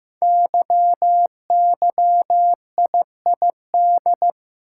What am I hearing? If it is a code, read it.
Morse code, heard as a keyed tone, about 15 wpm, 700 Hz. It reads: YYIID
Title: Code Groups - 5 Letter:Numbers - 15wpm